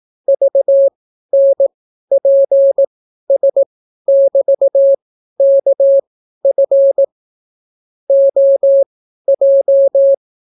Morse code, heard as a keyed tone, about 18 wpm, spaced as for 12 wpm, 550 Hz. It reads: VNPS<BT>KF OJ